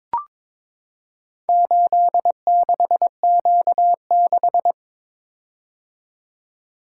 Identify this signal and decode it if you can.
Morse code, heard as a keyed tone, about 23 wpm, 700 Hz. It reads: 86Q6
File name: practice / 22wpm / Four Letter-Number Combinations - 22wpm